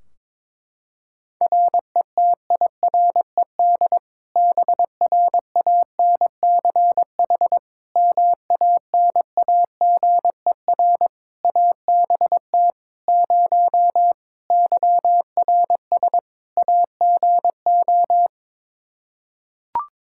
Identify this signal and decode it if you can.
Morse code, heard as a keyed tone, about 21 wpm, 700 Hz. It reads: RETIRED BRANCH MANAGER ABT 0 YRS AGO
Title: Practice QSO - Exchange 3 - More Info - 22wpm